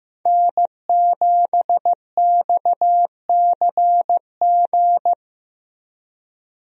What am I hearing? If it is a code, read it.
Morse code, heard as a keyed tone, about 15 wpm, 700 Hz. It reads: N7XCG
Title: USA Call Signs - 5 Characters - Repeated 3x - 15wpm